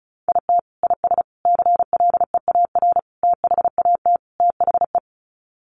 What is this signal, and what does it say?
Morse code, heard as a keyed tone, about 34 wpm, 700 Hz. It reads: IT IS CLEAR THAT THE
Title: Sets of 5 Words Encore 35wpm